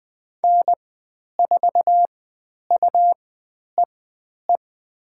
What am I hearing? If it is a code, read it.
Morse code, heard as a keyed tone, about 20 wpm, spaced as for 10 wpm, 700 Hz. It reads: N4UEE
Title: 109 Lesson 18g - Letter m - Call Signs 20wpm